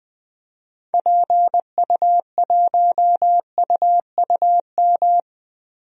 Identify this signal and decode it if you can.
Morse code, heard as a keyed tone, about 20 wpm, 700 Hz. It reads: PU1UUM
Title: DXCC-100 Call Signs - 6 Characters - Repeated 2x - 20wpm